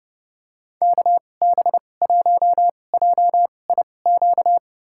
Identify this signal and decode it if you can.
Morse code, heard as a keyed tone, about 32 wpm, spaced as for 22 wpm, 700 Hz. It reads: KB1JIQ
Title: USA Call Signs - 6 Characters - 2x Character Spacing - 30wpm